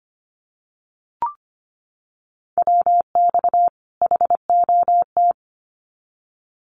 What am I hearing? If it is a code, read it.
Morse code, heard as a keyed tone, about 25 wpm, 700 Hz. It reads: WX HOT
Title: Common QSO Phrases 25wpm